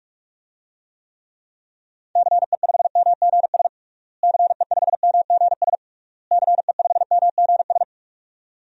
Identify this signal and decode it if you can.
Morse code, heard as a keyed tone, about 46 wpm, 700 Hz. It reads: CE5MGS CE5MGS CE5MGS
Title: DXCC-100 Call Signs - 6 Characters - Repeated 3x - 45wpm